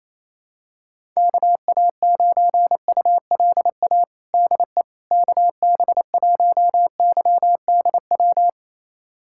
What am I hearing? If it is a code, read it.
Morse code, heard as a keyed tone, about 27 wpm, 700 Hz. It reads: KA9ULA DE KB1YDW
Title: USA Callsign DE Callsign - 6 Characters - 28wpm